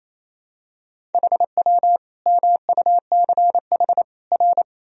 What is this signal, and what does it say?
Morse code, heard as a keyed tone, about 28 wpm, 700 Hz. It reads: HW MUCH R
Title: Sentences from Top 200 Words and Abbreviations 28wpm